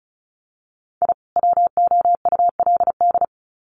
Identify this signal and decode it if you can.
Morse code, heard as a keyed tone, about 34 wpm, 700 Hz. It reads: I WOULD
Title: Sets of 2 Words 35wpm